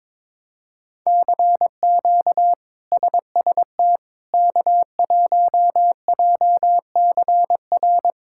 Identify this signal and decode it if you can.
Morse code, heard as a keyed tone, about 22 wpm, 700 Hz. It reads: CQ SST K1JCR
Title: K1USN SST QSO Practice - 22wpm